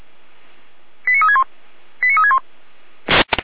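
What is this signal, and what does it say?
Five-tone selective call (ZVEI), heard as quick runs of five tones, tones of about 70 ms. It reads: 89361 89361
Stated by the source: ZVEI five-tone call 89356 received by SDR.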